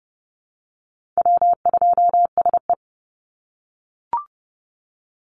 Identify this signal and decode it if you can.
Morse code, heard as a keyed tone, about 30 wpm, 700 Hz. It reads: W2SE
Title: NA SOTA Chaser Callsigns 30wpm